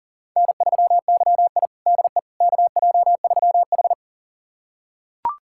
Morse code, heard as a keyed tone, about 40 wpm, 700 Hz. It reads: N3YI DE KJ3H
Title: USA Callsign DE Callsign - 4 Characters - 40wpm